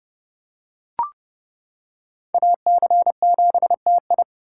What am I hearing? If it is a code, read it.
Morse code, heard as a keyed tone, about 28 wpm, 700 Hz. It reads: AC7TI